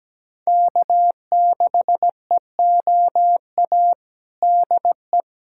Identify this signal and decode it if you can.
Morse code, heard as a keyed tone, about 17 wpm, 700 Hz. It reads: K6EOA DE
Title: USA Callsign DE Callsign - 5 Characters - 17wpm